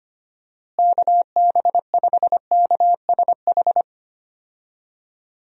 Morse code, heard as a keyed tone, about 25 wpm, 700 Hz. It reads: KB5KSH
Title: USA Call Signs - 6 Characters - Repeated 2x - 25wpm